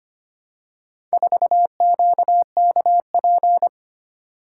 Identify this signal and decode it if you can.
Morse code, heard as a keyed tone, about 25 wpm, 700 Hz. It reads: 4QKP
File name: practice / 25wpm / Four Letter-Number Combinations - 25wpm